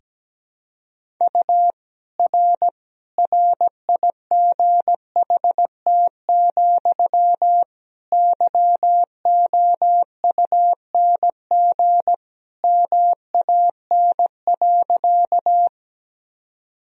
Morse code, heard as a keyed tone, about 17 wpm, 700 Hz. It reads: U R RIGHT, YOUNG MAN.
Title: Sentences from Top 400 Words and Abbreviations 17wpm